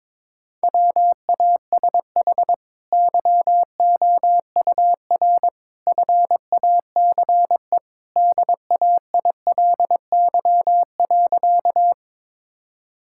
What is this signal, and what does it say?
Morse code, heard as a keyed tone, about 22 wpm, 700 Hz. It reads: WASH YOUR FACE DAILY.
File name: practice / 22wpm / Easy Sentences with Hints - 22wpm